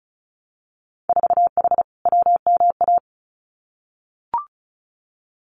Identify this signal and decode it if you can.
Morse code, heard as a keyed tone, about 34 wpm, 700 Hz. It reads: 4H WMA
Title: Winter Field Day Exchanges Encore 35wpm